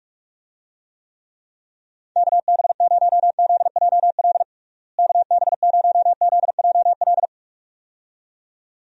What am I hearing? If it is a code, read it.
Morse code, heard as a keyed tone, about 46 wpm, 700 Hz. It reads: KB0ZJL KB0ZJL
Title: USA Call Signs - 6 Characters - Repeated 2x - 45wpm